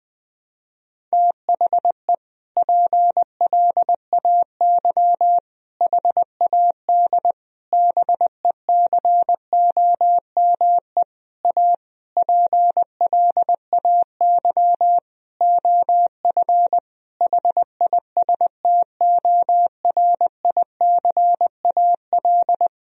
Morse code, heard as a keyed tone, about 21 wpm, 700 Hz. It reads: THE PLAY HAD BECOME A PLAY OF HISTORICAL
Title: Sentences from Top 1000 Words - Review Part 4 - 20wpm